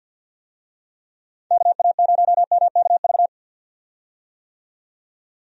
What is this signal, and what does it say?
Morse code, heard as a keyed tone, about 51 wpm, 700 Hz. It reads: KA0MKV
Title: USA Call Signs - 6 Characters - 50wpm